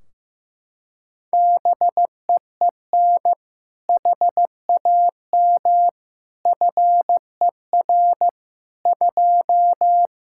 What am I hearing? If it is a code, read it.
Morse code, heard as a keyed tone, about 16 wpm, 700 Hz. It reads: BEEN HAM FER 2
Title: Practice QSO - Exchange 3 - More Info - 15wpm